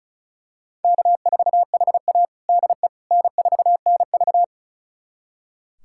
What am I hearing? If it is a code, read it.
Morse code, heard as a keyed tone, about 34 wpm, 700 Hz. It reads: K4HA DE N4NV
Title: USA Callsign DE Callsign - 4 Characters - 35wpm